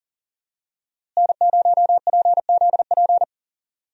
Morse code, heard as a keyed tone, about 40 wpm, 700 Hz. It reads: N0PZP
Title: USA Call Signs - 5 Characters - Repeated 3x - 40wpm